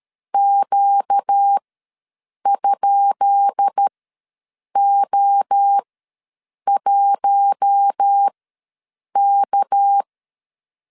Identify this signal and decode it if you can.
Morse code, heard as a keyed tone, about 13 wpm, spaced as for 7 wpm, 800 Hz. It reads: Q?O1K
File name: LEZIONI / 10 SIGNS / 10B INT ws 12wpm 3527